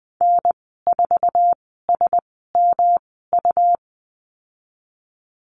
Morse code, heard as a keyed tone, about 20 wpm, spaced as for 15 wpm, 700 Hz. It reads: N4SMU
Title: 154 Lesson 25g - Letter p - Call Signs 20wpm